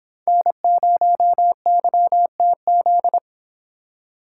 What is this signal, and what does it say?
Morse code, heard as a keyed tone, about 26 wpm, 700 Hz. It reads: N0YTZ
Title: Speed Racing - USA Call Signs 22wpm